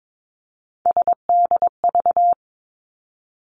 Morse code, heard as a keyed tone, about 22 wpm, 700 Hz. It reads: SDV
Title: Three Challenging Letter-Number Combinations 22wpm